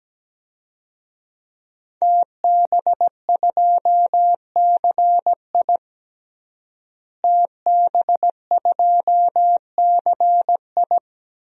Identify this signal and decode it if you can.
Morse code, heard as a keyed tone, about 17 wpm, 700 Hz. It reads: TB2CI TB2CI
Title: DXCC-100 Call Signs - 5 Characters - Repeated 2x - 17wpm